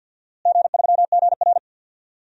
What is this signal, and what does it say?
Morse code, heard as a keyed tone, about 50 wpm, 700 Hz. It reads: G3GR
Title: DXCC-100 Call Signs - 4 Characters - Repeated 2x - 50wpm